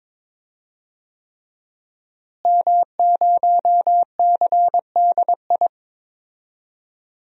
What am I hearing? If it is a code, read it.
Morse code, heard as a keyed tone, about 22 wpm, 700 Hz. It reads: M0CDI
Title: DXCC-100 Call Signs - 5 Characters - Repeated 3x - 22wpm